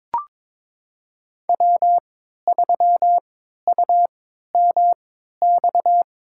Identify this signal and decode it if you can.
Morse code, heard as a keyed tone, about 22 wpm, spaced as for 13 wpm, 700 Hz. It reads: W3UMX
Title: USA Call Signs - 5 Characters - 3x Character Spacing - 22wpm